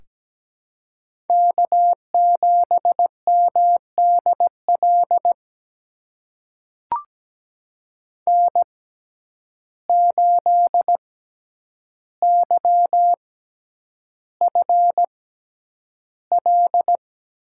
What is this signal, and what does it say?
Morse code, heard as a keyed tone, about 17 wpm, spaced as for 9 wpm, 700 Hz. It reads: K7MDL N8YFL
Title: USA Call Signs - 5 Characters - 6x Character Spacing - 17wpm